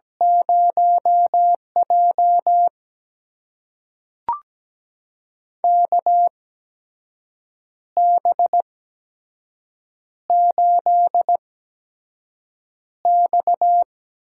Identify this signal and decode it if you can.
Morse code, heard as a keyed tone, about 17 wpm, spaced as for 6 wpm, 700 Hz. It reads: 0J KB8X
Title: USA Call Signs - 8x Character Spacing - 17wpm